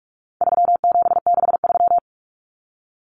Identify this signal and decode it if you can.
Morse code, heard as a keyed tone, about 44 wpm, 700 Hz. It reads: 3763
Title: Common Cut Numbers - 4-Digit 45wpm